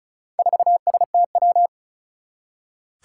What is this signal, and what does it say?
Morse code, heard as a keyed tone, about 36 wpm, 700 Hz. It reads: 4STW
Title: Four Challenging Letter-Number Combinations 35wpm